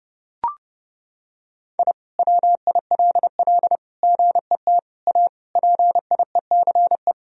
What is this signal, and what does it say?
Morse code, heard as a keyed tone, about 32 wpm, 700 Hz. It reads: I WILL GET A PIECE
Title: Sentences from Top 1000 Words - Review Part 2 - 30wpm